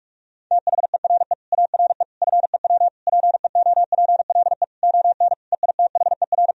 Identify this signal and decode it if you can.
Morse code, heard as a keyed tone, about 46 wpm, 700 Hz. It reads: THERE ARE FEW PEOPLE ON EITHER
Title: Sentences from Top 1000 Words - Review Part 4 - 45wpm